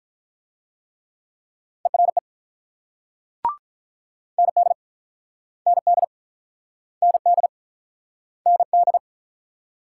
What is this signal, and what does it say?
Morse code, heard as a keyed tone, about 45 wpm, 700 Hz. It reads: ERE ND ND ND ND